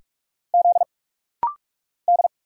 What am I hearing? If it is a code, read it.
Morse code, heard as a keyed tone, about 46 wpm, 700 Hz. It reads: Z D
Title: Single Letters - Rapid-Fire 45wpm